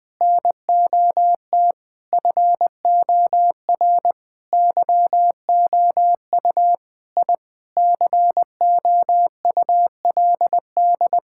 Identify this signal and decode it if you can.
Morse code, heard as a keyed tone, about 20 wpm, 700 Hz. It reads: NOT FOR YOU I COULD
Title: Sentences with 5-Word Phrases - 20wpm